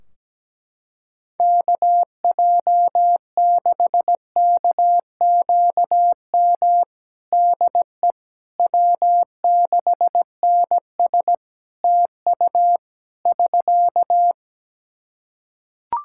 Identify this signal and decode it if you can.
Morse code, heard as a keyed tone, about 17 wpm, 700 Hz. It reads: KJ6KQM DE W6NS TU <SK>
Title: Practice QSO - Exchange 4 - Ending - 17wpm